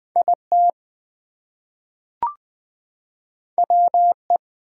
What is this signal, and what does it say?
Morse code, heard as a keyed tone, about 20 wpm, 700 Hz. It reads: IT WE
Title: Two Letter Words - 20wpm